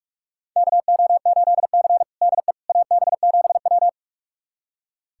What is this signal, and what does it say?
Morse code, heard as a keyed tone, about 46 wpm, 700 Hz. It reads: KO8C DE AB7W
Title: USA Callsign DE Callsign - 4 Characters - 45wpm